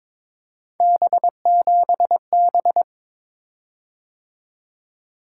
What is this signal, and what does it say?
Morse code, heard as a keyed tone, about 22 wpm, 700 Hz. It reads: B7B
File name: practice / 22wpm / Three Letter-Number Combinations - 22wpm